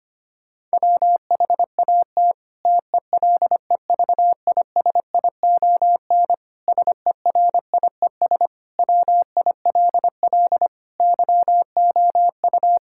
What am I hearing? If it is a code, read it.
Morse code, heard as a keyed tone, about 25 wpm, 700 Hz. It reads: WHAT TELEVISION SERIES WILL YOU
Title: Sentences from Top 1000 Words - Review Part 5 - 25wpm